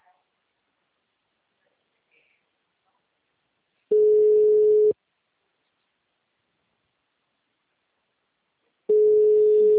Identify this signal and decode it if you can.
Ringback tone (Europe, 425 Hz).
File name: out- -2031- - - .1561